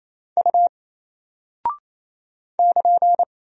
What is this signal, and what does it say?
Morse code, heard as a keyed tone, about 28 wpm, 700 Hz. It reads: U <KN>
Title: Single Character (Letter:Number:Punctuation:Prosign) 28wpm